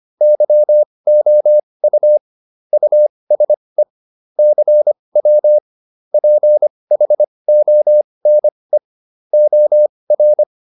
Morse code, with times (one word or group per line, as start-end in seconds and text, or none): 0.2-2.2 YOU
2.7-3.8 USE
4.4-5.6 CW
6.1-8.8 PHONE
9.3-10.4 OR